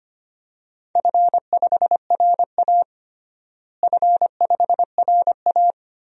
Morse code, heard as a keyed tone, about 25 wpm, 700 Hz. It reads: F5RA F5RA
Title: DXCC-100 Call Signs - 4 Characters - Repeated 2x - 25wpm